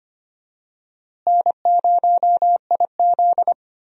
Morse code, heard as a keyed tone, about 25 wpm, 700 Hz. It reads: N0IZ
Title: USA Call Signs - 4 Characters - Repeated 3x - 25wpm